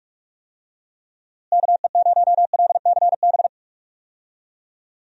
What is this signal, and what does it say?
Morse code, heard as a keyed tone, about 41 wpm, 700 Hz. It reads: KE0LCB
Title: USA Call Signs - 6 Characters - Repeated 3x - 45wpm